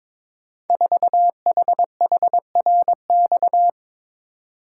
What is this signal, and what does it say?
Morse code, heard as a keyed tone, about 22 wpm, 700 Hz. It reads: 4HHRX
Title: Code Groups - 5 Letter:Numbers - 22wpm